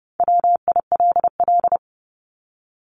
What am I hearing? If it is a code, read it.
Morse code, heard as a keyed tone, about 30 wpm, 700 Hz. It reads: WILL
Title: Top 200 Words Repeated 30wpm